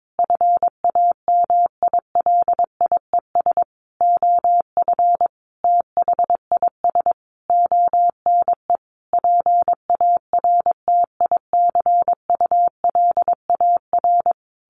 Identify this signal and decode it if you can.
Morse code, heard as a keyed tone, about 22 wpm, 700 Hz. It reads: FAMILIES OF THIS ONE PARTICULAR
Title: A Sound Of Thunder 22wpm